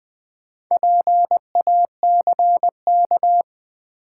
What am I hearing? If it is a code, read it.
Morse code, heard as a keyed tone, about 20 wpm, 700 Hz. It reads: PACK
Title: Sentences from Top 1300 Words - New Word Drill - 20wpm